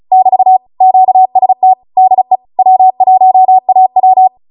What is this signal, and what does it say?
Morse code, heard as a keyed tone, about 36 wpm, 750 Hz. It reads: <BT> QST DE W1AW
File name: Morse / 35WPM